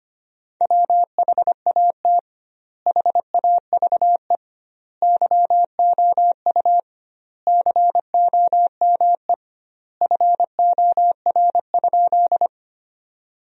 Morse code, heard as a keyed tone, about 25 wpm, 700 Hz. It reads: WHAT HAVE YOU COME FOR?
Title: 2x Word-Spacing - Sentences from Top 100 Words - 25wpm